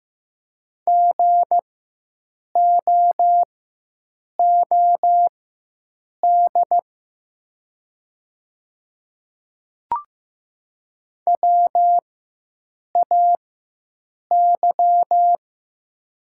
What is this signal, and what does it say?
Morse code, heard as a keyed tone, about 15 wpm, spaced as for 7 wpm, 700 Hz. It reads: GOOD WAY
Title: 4x Character Spacing - Top 200 Words - 15wpm